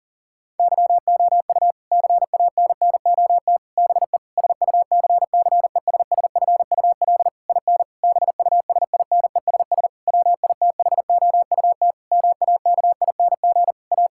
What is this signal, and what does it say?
Morse code, heard as a keyed tone, about 40 wpm, 700 Hz. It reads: YOU CANNOT BE SUCCESSFUL IN BUSINESS WITHOUT MAKING A
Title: Sentences from Top 1000 Words - Review Part 6 - 40wpm